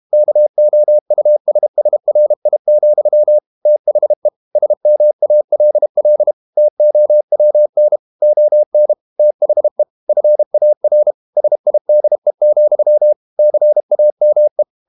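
Morse code, heard as a keyed tone, about 32 wpm, 600 Hz. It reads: KOUSSRI, THE SMALL TOWN ON THE FAR SIDE, CAME
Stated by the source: Morse code practice at 32 wpm from CW med Gnister, reading Ken Follett's novel Never in English.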